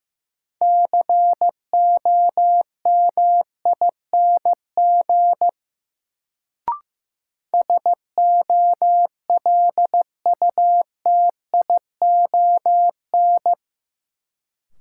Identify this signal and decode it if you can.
Morse code, heard as a keyed tone, about 15 wpm, 700 Hz. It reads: COMING SOLUTION
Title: Sentences from Top 1300 Words - New Word Drill Alt - 15wpm